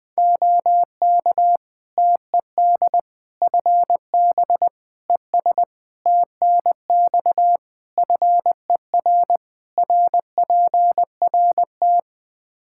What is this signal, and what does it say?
Morse code, heard as a keyed tone, about 20 wpm, 700 Hz. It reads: OK TED FB ES TNX FER RPRT
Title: Practice QSO - Exchange 2 - Info - 20wpm